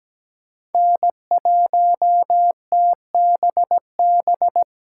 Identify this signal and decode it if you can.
Morse code, heard as a keyed tone, about 17 wpm, 700 Hz. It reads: N1TBB
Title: USA Call Signs - 5 Characters - 3x Character Spacing - 17wpm